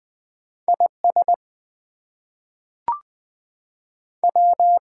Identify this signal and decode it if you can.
Morse code, heard as a keyed tone, about 20 wpm, 700 Hz. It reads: IS W
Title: Two Letter Words - 20wpm